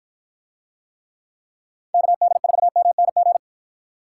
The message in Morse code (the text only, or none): KD4MNG